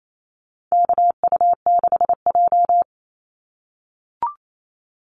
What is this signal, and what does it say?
Morse code, heard as a keyed tone, about 28 wpm, 700 Hz. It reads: KU6J
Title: NA SOTA Activator Callsigns 28wpm